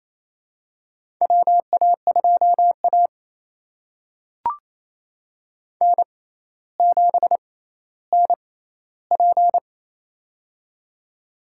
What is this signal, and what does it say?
Morse code, heard as a keyed tone, about 28 wpm, spaced as for 14 wpm, 700 Hz. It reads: WA2A N7NP
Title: USA Call Signs - 4 Characters - 6x Character Spacing - 28wpm